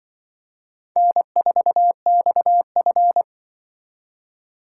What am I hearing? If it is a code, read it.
Morse code, heard as a keyed tone, about 24 wpm, 700 Hz. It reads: N4XF
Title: Speed Racing - USA 1x2 Call Signs 20wpm